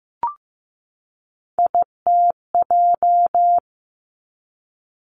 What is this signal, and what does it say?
Morse code, heard as a keyed tone, about 15 wpm, 700 Hz. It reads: ITJ
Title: Three Challenging Letter-Number Combinations 15wpm